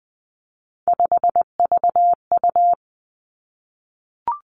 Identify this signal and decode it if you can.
Morse code, heard as a keyed tone, about 20 wpm, 700 Hz. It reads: 5VU
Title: Three Challenging Letter-Number Combinations 20wpm